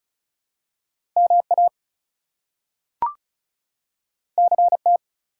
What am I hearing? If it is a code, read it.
Morse code, heard as a keyed tone, about 36 wpm, 700 Hz. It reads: MA CT
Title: US States and Canadian Provinces 35wpm